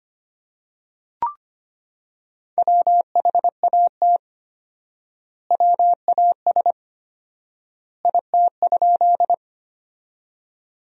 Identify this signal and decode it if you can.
Morse code, heard as a keyed tone, about 25 wpm, 700 Hz. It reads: WHAT WAS IT?
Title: 4x Word-Spacing - Sentences from Top 100 Words - 25wpm